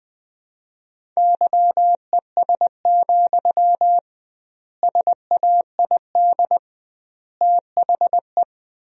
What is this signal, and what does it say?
Morse code, heard as a keyed tone, about 21 wpm, 700 Hz. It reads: YES, SAID THE
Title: 2x Word-Spacing - Sentences from Top 500 Words - 20wpm